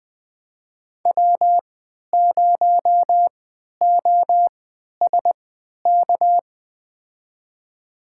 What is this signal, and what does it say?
Morse code, heard as a keyed tone, about 21 wpm, spaced as for 12 wpm, 700 Hz. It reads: W0OSK
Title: USA Call Signs - 3x Character Spacing - 20wpm